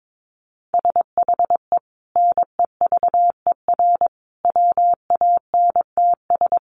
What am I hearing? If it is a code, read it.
Morse code, heard as a keyed tone, about 22 wpm, 700 Hz. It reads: SHE NEVER WANTS
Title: Easy Phrases 22wpm